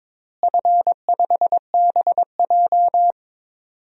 Morse code, heard as a keyed tone, about 22 wpm, 700 Hz. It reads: F5BJ
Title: DXCC-100 Call Signs - 4 Characters - Repeated 2x - 22wpm